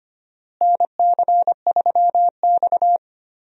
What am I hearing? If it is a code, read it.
Morse code, heard as a keyed tone, about 25 wpm, 700 Hz. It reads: NC3X
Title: USA Call Signs - 4 Characters - Repeated 2x - 25wpm